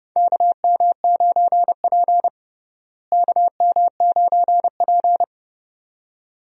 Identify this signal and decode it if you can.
Morse code, heard as a keyed tone, about 30 wpm, 700 Hz. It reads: KM9P KM9P
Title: USA Call Signs - 4 Characters - Repeated 2x - 30wpm